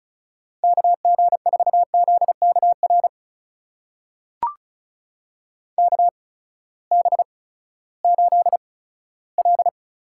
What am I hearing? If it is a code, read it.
Morse code, heard as a keyed tone, about 34 wpm, spaced as for 17 wpm, 700 Hz. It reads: KG4ZKR KB8L
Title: USA Call Signs - 6 Characters - 8x Character Spacing - 35wpm